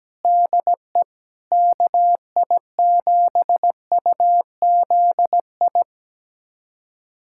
Morse code, heard as a keyed tone, about 17 wpm, 700 Hz. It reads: DE KI7UZI
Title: USA Callsign DE Callsign - 6 Characters - 17wpm